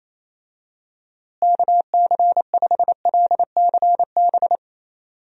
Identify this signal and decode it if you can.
Morse code, heard as a keyed tone, about 28 wpm, 700 Hz. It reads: KC5LCB
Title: USA Call Signs - 6 Characters - Repeated 2x - 28wpm